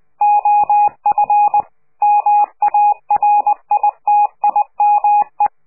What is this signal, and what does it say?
Morse code, heard as a keyed tone, about 20 wpm, 830 Hz. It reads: OF MARITIME
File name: CQ de PCH laatste bericht farewell